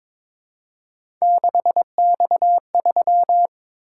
Morse code, heard as a keyed tone, about 21 wpm, 700 Hz. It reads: 6X3
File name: practice / 22wpm / Three Letter-Number Combinations - 22wpm